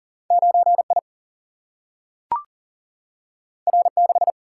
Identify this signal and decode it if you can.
Morse code, heard as a keyed tone, about 40 wpm, 700 Hz. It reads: 9 I R 6
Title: Single-Letter Single-Number 40wpm